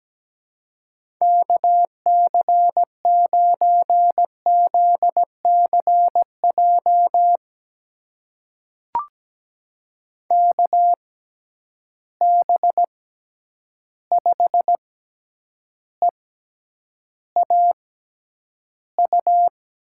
Morse code, heard as a keyed tone, about 17 wpm, spaced as for 9 wpm, 700 Hz. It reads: KC9ZCJ KB5EAU
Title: USA Call Signs - 6 Characters - 6x Character Spacing - 17wpm